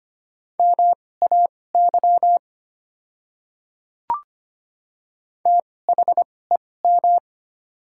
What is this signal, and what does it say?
Morse code, heard as a keyed tone, about 25 wpm, spaced as for 18 wpm, 700 Hz. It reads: MAY THEM
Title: 2x Character Spacing - Top 100 Words - 25wpm